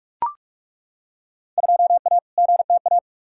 Morse code, heard as a keyed tone, about 41 wpm, 700 Hz. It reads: 2A GTA
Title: Field Day Exchanges Encore - 45wpm